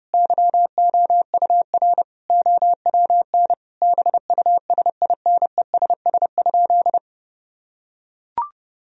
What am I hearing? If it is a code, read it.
Morse code, heard as a keyed tone, about 30 wpm, 700 Hz. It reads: YOUR OWN BUSINESS?
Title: Sentences from Top 1000 Words - Review Part 4 - 30wpm